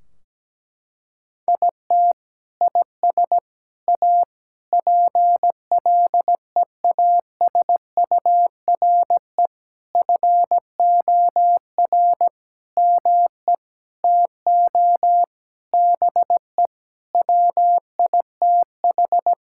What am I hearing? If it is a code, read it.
Morse code, heard as a keyed tone, about 17 wpm, 700 Hz. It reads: IT IS A PLEASURE FOR ME TO BE WITH
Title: Sentences from Top 1600 Words 17wpm